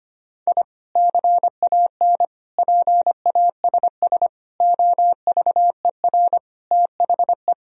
Morse code, heard as a keyed tone, about 25 wpm, 700 Hz. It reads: I CAN PASS OVER THE
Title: Sentences from Top 1000 Words - Review Part 1 - 25wpm